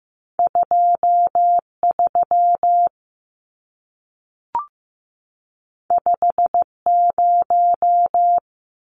Morse code, heard as a keyed tone, about 15 wpm, 700 Hz. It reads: 23 50
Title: Numbers - 2-Digit 15wpm